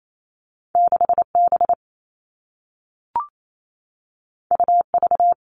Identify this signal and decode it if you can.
Morse code, heard as a keyed tone, about 28 wpm, 700 Hz. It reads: 6B UV
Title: All Cut Numbers - 2-Digit 28wpm